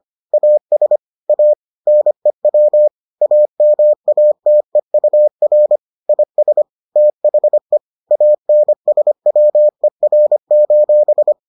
Morse code, heard as a keyed tone, about 25 wpm, 600 Hz. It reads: AS A NEW AMATEUR IS THE ANSWER: